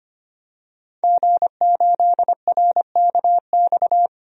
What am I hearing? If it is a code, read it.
Morse code, heard as a keyed tone, about 25 wpm, 700 Hz. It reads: G8RKX
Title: DXCC-100 Call Signs - 5 Characters - 25wpm